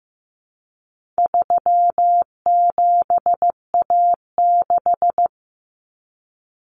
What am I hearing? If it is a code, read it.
Morse code, heard as a keyed tone, about 15 wpm, 700 Hz. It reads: 37A6
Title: Common Cut Numbers - 4-Digit 15wpm